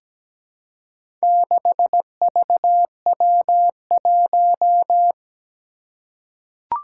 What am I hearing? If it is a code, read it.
Morse code, heard as a keyed tone, about 17 wpm, 700 Hz. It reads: 6VW1
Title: Four Challenging Letter-Number Combinations 17wpm